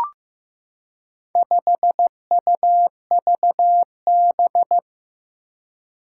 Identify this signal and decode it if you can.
Morse code, heard as a keyed tone, about 15 wpm, 700 Hz. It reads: 5UVB
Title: Four Challenging Letter-Number Combinations 15wpm